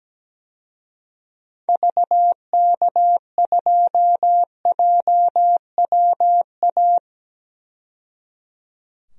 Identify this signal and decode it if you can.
Morse code, heard as a keyed tone, about 17 wpm, 700 Hz. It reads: VK2JWA
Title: DXCC-100 Call Signs - 6 Characters - 17wpm